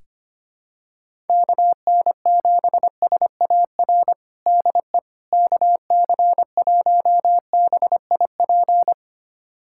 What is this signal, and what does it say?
Morse code, heard as a keyed tone, about 25 wpm, 700 Hz. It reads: KN7SAR DE KC1BIP
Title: USA Callsign DE Callsign 25wpm